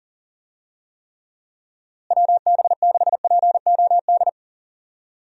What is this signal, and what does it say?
Morse code, heard as a keyed tone, about 40 wpm, 700 Hz. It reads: WB6POD
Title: USA Call Signs - 6 Characters - Repeated 3x - 40wpm